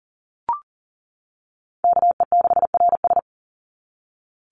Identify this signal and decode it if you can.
Morse code, heard as a keyed tone, about 40 wpm, 700 Hz. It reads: KE6RS
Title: NA SOTA Chaser Callsigns 40wpm